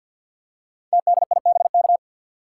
Morse code, heard as a keyed tone, about 53 wpm, 700 Hz. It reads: TDIBK
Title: Code Groups - 5 Letters - 50wpm